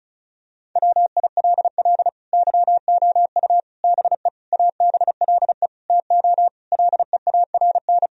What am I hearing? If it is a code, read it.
Morse code, heard as a keyed tone, about 36 wpm, 700 Hz. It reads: WILL YOU BE ABLE TO LEARN
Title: Sentences from Top 1000 Words - Review Part 2 - 35wpm